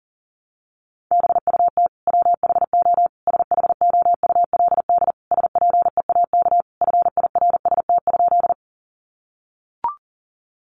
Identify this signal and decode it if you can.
Morse code, heard as a keyed tone, about 40 wpm, 700 Hz. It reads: BUT WHO SHOULD SPEAK FIRST?
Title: Sentences from Top 1100 Words 40wpm